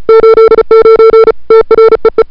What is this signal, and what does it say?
Morse code, heard as a keyed tone, about 34 wpm, 440 Hz. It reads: 89 TREE
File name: CWSprint / February 2007 CW Sprint / 01-2007 CW Sprint Hour1